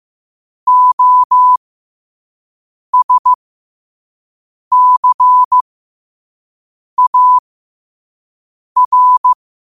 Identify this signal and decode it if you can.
Morse code, heard as a keyed tone, about 15 wpm, spaced as for 5 wpm, 1000 Hz. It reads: OSCAR
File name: qotdfiles / 5wpm